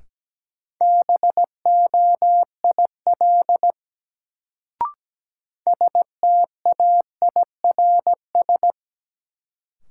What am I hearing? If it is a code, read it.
Morse code, heard as a keyed tone, about 17 wpm, 700 Hz. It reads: BOIL STAIRS
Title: Sets of 5 Words Encore - New Word Drill - 17wpm